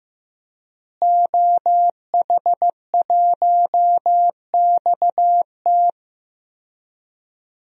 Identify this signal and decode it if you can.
Morse code, heard as a keyed tone, about 16 wpm, 700 Hz. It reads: OH1XT
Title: DXCC-100 Call Signs - 5 Characters - Repeated 3x - 15wpm